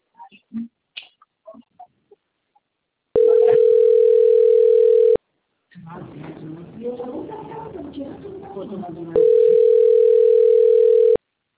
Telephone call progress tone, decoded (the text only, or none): ringback tone